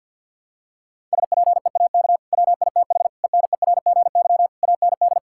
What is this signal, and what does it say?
Morse code, heard as a keyed tone, about 50 wpm, 700 Hz. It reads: SPEAK WITH ENERGY AND
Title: Sentences from Top 1000 Words - Review Part 6 - 50wpm